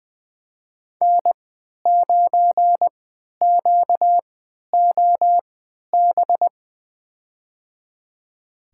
Morse code, heard as a keyed tone, about 20 wpm, spaced as for 11 wpm, 700 Hz. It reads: N9QOB
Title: USA Call Signs - 5 Characters - 3x Character Spacing - 20wpm